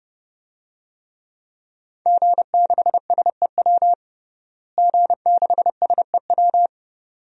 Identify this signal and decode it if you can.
Morse code, heard as a keyed tone, about 30 wpm, 700 Hz. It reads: G6SEW G6SEW
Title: DXCC-100 Call Signs - 5 Characters - Repeated 2x - 30wpm